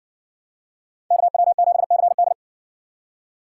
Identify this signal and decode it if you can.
Morse code, heard as a keyed tone, about 60 wpm, 700 Hz. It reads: KW7CD